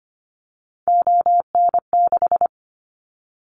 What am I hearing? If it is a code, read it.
Morse code, heard as a keyed tone, about 25 wpm, 700 Hz. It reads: ON6
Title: Callsign Prefixes for Top 100 DXCC - 25wpm